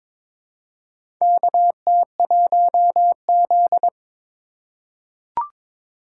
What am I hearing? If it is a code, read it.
Morse code, heard as a keyed tone, about 22 wpm, 700 Hz. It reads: KT1Z
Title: USA Call Signs - 4 Characters - 3x Character Spacing - 22wpm